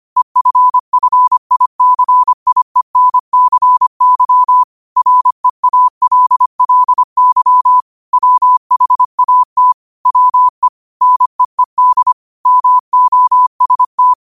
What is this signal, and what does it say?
Morse code, heard as a keyed tone, about 25 wpm, 1000 Hz. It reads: EFFICIENCY REALLY WHAT WE NEED MOST
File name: qotdfiles / 25wpm